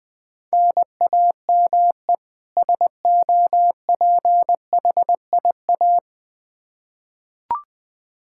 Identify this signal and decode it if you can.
Morse code, heard as a keyed tone, about 19 wpm, 700 Hz. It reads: NAME SOPHIA
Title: Practice QSO - Exchange 1 - Beginning - 20wpm